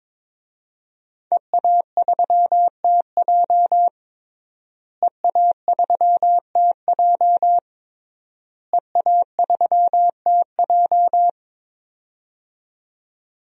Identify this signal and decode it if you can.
Morse code, heard as a keyed tone, about 22 wpm, 700 Hz. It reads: EA3TJ EA3TJ EA3TJ
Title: DXCC-100 Call Signs - 5 Characters - Repeated 3x - 22wpm